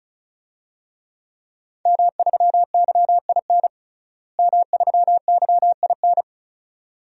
Morse code, heard as a keyed tone, about 36 wpm, 700 Hz. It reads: M3YIN M3YIN
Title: DXCC-100 Call Signs - 5 Characters - Repeated 2x - 35wpm